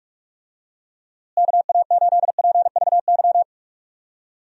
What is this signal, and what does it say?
Morse code, heard as a keyed tone, about 43 wpm, 700 Hz. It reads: KA8PVY
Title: USA Call Signs - 6 Characters - Repeated 3x - 45wpm